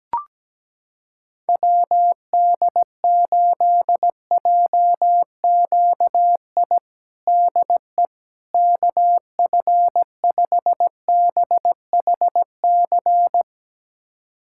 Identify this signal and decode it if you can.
Morse code, heard as a keyed tone, about 17 wpm, 700 Hz. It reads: WD8JQI DE KF5BHC
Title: USA Callsign DE Callsign 17wpm